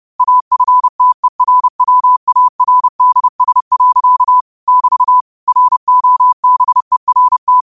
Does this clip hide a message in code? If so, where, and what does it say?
Yes: Morse code from 0.2 to 7.6 s, heard as a keyed tone, about 30 wpm, 1000 Hz. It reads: AFTERWARDS. <BT> ROBERT